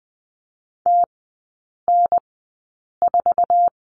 Morse code, heard as a keyed tone, about 20 wpm, spaced as for 8 wpm, 700 Hz. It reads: TN4
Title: 047 Lesson 9e - Number 4 - Three Characters 20wpm